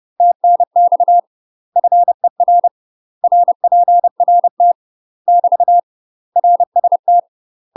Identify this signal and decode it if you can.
Morse code, heard as a keyed tone, about 30 wpm, 700 Hz. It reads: TNX FER RPRT <BT> RST
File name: UK MASTER QSOs TRAINING / 30WPM / UK MASTER QSOs 0033